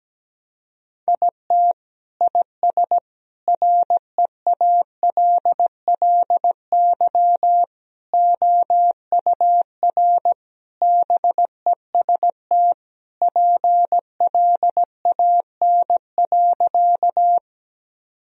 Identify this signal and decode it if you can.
Morse code, heard as a keyed tone, about 17 wpm, 700 Hz. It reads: IT IS REALLY OUR BEST PLAN.
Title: Sentences from Top 1000 Words - Review Part 2 - 17wpm